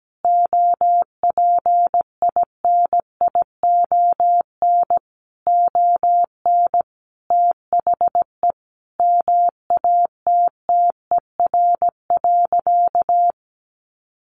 Morse code, heard as a keyed tone, about 17 wpm, 700 Hz. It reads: OPINION ON THE MATTER.
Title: Sentences from Top 1200 Words 17wpm